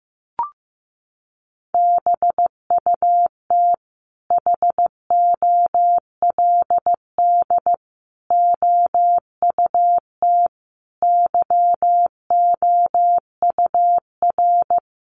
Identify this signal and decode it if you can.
Morse code, heard as a keyed tone, about 15 wpm, 700 Hz. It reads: BUT HOLD OUT YOUR
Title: Sentences from Top 1300 Words 15wpm